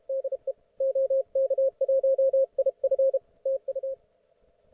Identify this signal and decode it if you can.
Morse code, heard as a keyed tone, about 32 wpm, 530 Hz, with radio noise. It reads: DE OK1IF TU
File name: Archiv / CUC182 / ok1fhi cul182